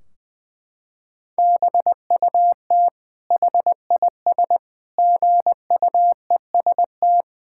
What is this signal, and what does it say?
Morse code, heard as a keyed tone, about 20 wpm, 700 Hz. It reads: BUT HIS GUEST
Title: Sentences from Top 1600 Words 20wpm